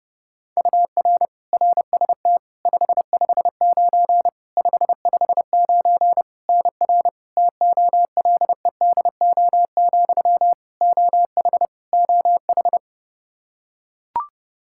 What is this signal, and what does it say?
Morse code, heard as a keyed tone, about 30 wpm, 700 Hz. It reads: UR RST 559 559 NR TOLEDO, OH OH
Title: Practice QSO - Exchange 1 - Beginning - 30wpm